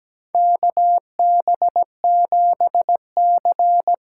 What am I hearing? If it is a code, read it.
Morse code, heard as a keyed tone, about 17 wpm, 700 Hz. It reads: KB7C
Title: USA Call Signs - 6 Characters - 8x Character Spacing - 17wpm